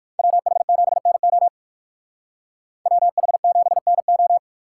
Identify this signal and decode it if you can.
Morse code, heard as a keyed tone, about 51 wpm, 700 Hz. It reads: WH7NO WH7NO
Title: Speed Racing - USA 2x2 Call Signs 45wpm